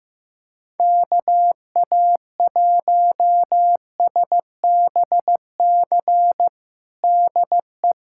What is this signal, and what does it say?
Morse code, heard as a keyed tone, about 15 wpm, 700 Hz. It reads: KA1SBC DE
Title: USA Callsign DE Callsign - 6 Characters - 15wpm